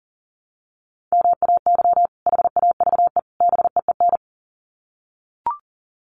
Morse code, heard as a keyed tone, about 40 wpm, 700 Hz. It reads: MAY HAVE BEEN
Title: Sets of 3 Words 40wpm